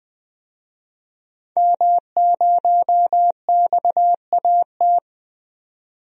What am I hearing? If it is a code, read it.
Morse code, heard as a keyed tone, about 20 wpm, 700 Hz. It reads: M0XAT
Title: DXCC-100 Call Signs - 5 Characters - 20wpm